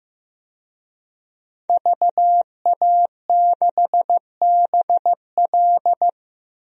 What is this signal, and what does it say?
Morse code, heard as a keyed tone, about 15 wpm, 700 Hz. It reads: VA6BL
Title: DXCC-100 Call Signs - 5 Characters - 15wpm